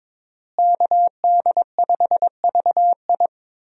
Morse code, heard as a keyed tone, about 22 wpm, 700 Hz. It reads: KD5VI
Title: USA Call Signs - 5 Characters - Repeated 3x - 22wpm